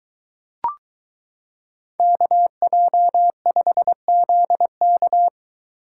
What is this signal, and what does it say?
Morse code, heard as a keyed tone, about 22 wpm, 700 Hz. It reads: KJ5ZK